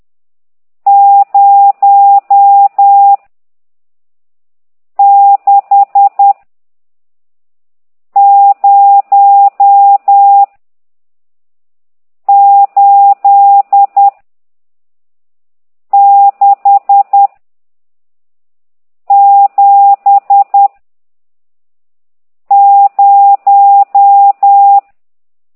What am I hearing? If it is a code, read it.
Morse code, heard as a keyed tone, about 10 wpm, 800 Hz. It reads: 0 6 0 8 6 7 0
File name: ct2idw / exerc14